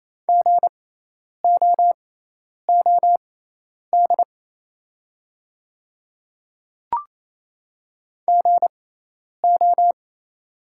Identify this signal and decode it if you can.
Morse code, heard as a keyed tone, about 28 wpm, spaced as for 10 wpm, 700 Hz. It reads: GOOD GO
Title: 6x Character Spacing - Top 200 Words - 28wpm